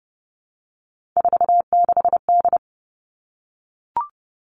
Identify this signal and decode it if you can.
Morse code, heard as a keyed tone, about 30 wpm, 700 Hz. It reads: 46D
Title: Three Challenging Letter-Number Combinations 30wpm